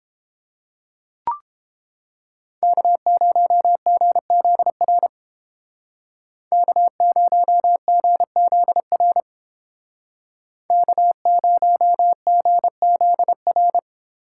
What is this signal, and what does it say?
Morse code, heard as a keyed tone, about 29 wpm, 700 Hz. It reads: K0GZR K0GZR K0GZR